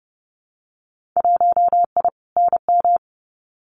Morse code, heard as a keyed tone, about 31 wpm, 700 Hz. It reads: 1I NM
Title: Winter Field Day Exchanges Encore 30wpm